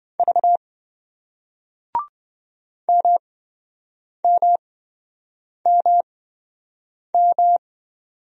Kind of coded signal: Morse code

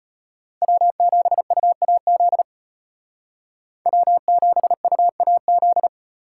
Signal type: Morse code

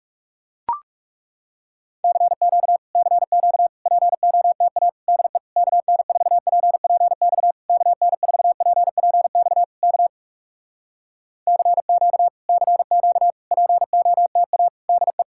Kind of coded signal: Morse code

